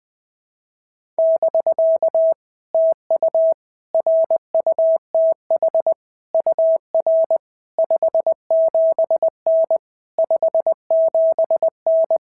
Morse code, heard as a keyed tone, about 20 wpm, 660 Hz. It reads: <BK> TU RUTH UR 57N 57N
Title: POTA QSO Practice - 20wpm